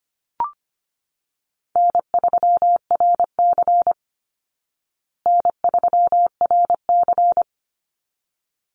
Morse code, heard as a keyed tone, about 25 wpm, 700 Hz. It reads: N3RC N3RC
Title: Easy Call Signs Repeated 2x - 25wpm